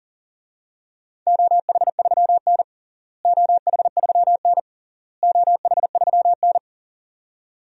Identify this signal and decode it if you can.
Morse code, heard as a keyed tone, about 40 wpm, 700 Hz. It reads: OH3N OH3N OH3N
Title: DXCC-100 Call Signs - 4 Characters - Repeated 3x - 40wpm